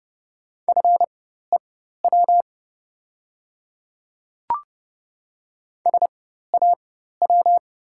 Morse code, heard as a keyed tone, about 30 wpm, spaced as for 14 wpm, 700 Hz. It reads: FEW SAW
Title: 4x Character Spacing - Top 300 Words - 30wpm